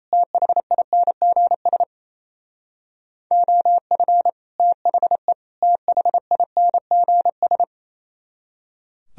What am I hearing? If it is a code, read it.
Morse code, heard as a keyed tone, about 36 wpm, 700 Hz. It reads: THINGS OF THE THINGS